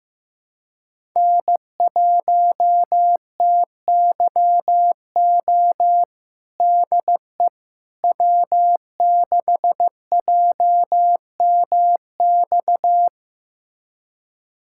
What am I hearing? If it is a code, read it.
Morse code, heard as a keyed tone, about 15 wpm, 700 Hz. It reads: N1TYO DE W6JMX
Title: USA Callsign DE Callsign - 5 Characters - 15wpm